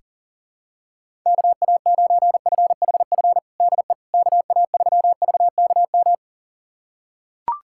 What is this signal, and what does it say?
Morse code, heard as a keyed tone, about 37 wpm, 700 Hz. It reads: KA9FHF DE KA3VKM
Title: USA Callsign DE Callsign - 6 Characters - 40wpm